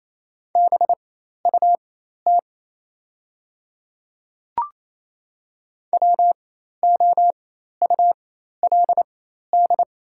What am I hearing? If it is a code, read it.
Morse code, heard as a keyed tone, about 28 wpm, spaced as for 13 wpm, 700 Hz. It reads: BUT WOULD
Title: 4x Character Spacing - Top 100 Words - 28wpm